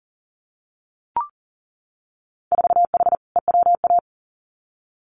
Morse code, heard as a keyed tone, about 40 wpm, 700 Hz. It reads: 4H EWA
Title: Winter Field Day Exchanges Encore 40wpm